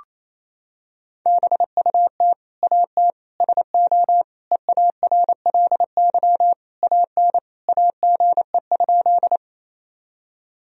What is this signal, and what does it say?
Morse code, heard as a keyed tone, about 27 wpm, 700 Hz. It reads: BUT AT SO EARLY AN AGE?
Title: Sentences from Top 1000 Words - Review Part 3 - 28wpm